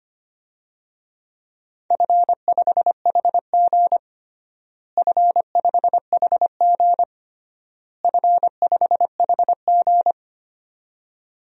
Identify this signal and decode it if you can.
Morse code, heard as a keyed tone, about 25 wpm, 700 Hz. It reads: F5HG F5HG F5HG
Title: DXCC-100 Call Signs - 4 Characters - Repeated 3x - 25wpm